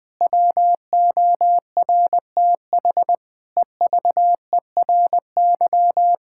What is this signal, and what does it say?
Morse code, heard as a keyed tone, about 21 wpm, 700 Hz. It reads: WORTH EVERY
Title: Difficult Sentences with Hints - 20wpm